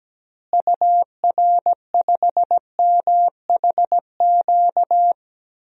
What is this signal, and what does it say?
Morse code, heard as a keyed tone, about 17 wpm, 700 Hz. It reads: UR5MHQ
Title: DXCC-100 Call Signs - 6 Characters - Repeated 3x - 17wpm